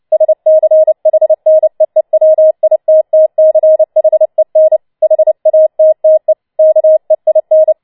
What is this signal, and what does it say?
Morse code, heard as a keyed tone, about 29 wpm, 630 Hz. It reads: SCHNEEWITTCHEN HATTE KEIN